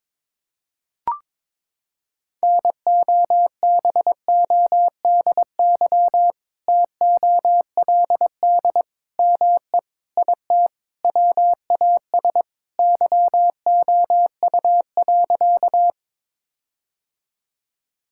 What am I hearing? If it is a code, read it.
Morse code, heard as a keyed tone, about 22 wpm, 700 Hz. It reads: NOBODY TOLD ME IT WAS YOU.
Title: Sentences from Top 1600 Words 22wpm